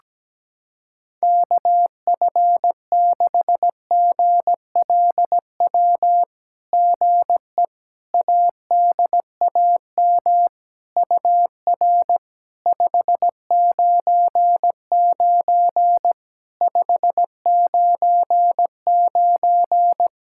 Morse code, heard as a keyed tone, about 17 wpm, 700 Hz. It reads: KF6GLW GE ADAM UR 599 599
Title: SOTA QSO Practice 17wpm